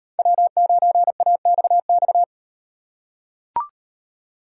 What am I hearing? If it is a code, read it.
Morse code, heard as a keyed tone, about 36 wpm, 700 Hz. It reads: W9AXX